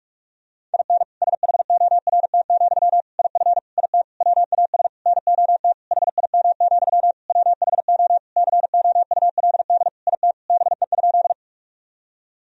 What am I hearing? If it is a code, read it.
Morse code, heard as a keyed tone, about 46 wpm, 700 Hz. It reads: IN SHORT, IF IT WAS NOT HIM, WHO COULD IT BE?
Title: Sentences from Top 1000 Words - Review Part 2 - 45wpm